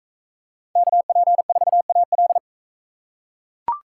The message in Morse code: KP4AL

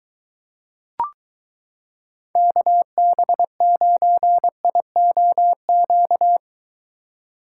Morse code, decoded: KB9IOQ

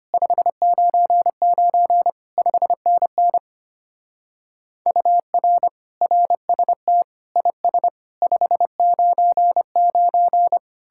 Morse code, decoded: 599 5NN UR RST IS 599

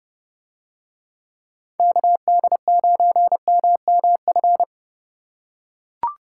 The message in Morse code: KD9MMF

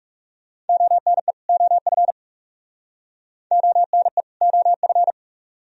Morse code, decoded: ONE OF ONE OF

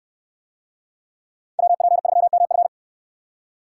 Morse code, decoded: WP3MF

